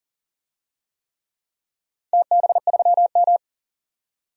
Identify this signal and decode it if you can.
Morse code, heard as a keyed tone, about 40 wpm, 700 Hz. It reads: TB3M
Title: DXCC-100 Call Signs - 4 Characters - 40wpm